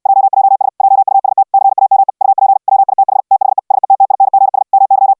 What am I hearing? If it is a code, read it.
Morse code, heard as a keyed tone, about 69 wpm, 780 Hz. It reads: JJS JUST CAME UP WITH THE SETTINGS MY